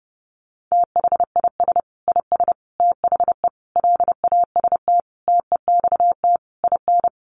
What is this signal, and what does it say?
Morse code, heard as a keyed tone, about 30 wpm, 700 Hz. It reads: THIS IS THE LAST TEXT IN
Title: Sentences from Top 1200 Words 30wpm